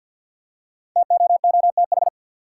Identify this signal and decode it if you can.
Morse code, heard as a keyed tone, about 47 wpm, 700 Hz. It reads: TOOTH
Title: Five Letter Words - 50wpm